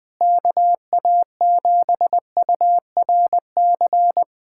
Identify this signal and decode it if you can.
Morse code, heard as a keyed tone, about 20 wpm, 700 Hz. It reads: KA7URC
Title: USA Callsign DE Callsign - 6 Characters - 20wpm